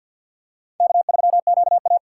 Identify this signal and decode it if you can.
Morse code, heard as a keyed tone, about 46 wpm, 700 Hz. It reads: K3QA
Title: USA Call Signs - 3x Character Spacing - 50wpm